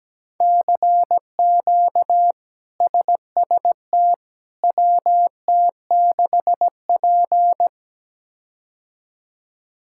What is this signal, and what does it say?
Morse code, heard as a keyed tone, about 17 wpm, 700 Hz. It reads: CQ SST WT6P
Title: K1USN SST QSO Practice - 17wpm